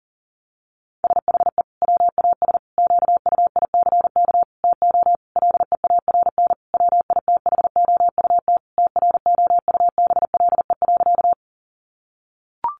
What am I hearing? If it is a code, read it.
Morse code, heard as a keyed tone, about 40 wpm, 700 Hz. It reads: SHE WAS QUICK TO LEARN WITHOUT TROUBLE.
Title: Sentences from Top 1200 Words 40wpm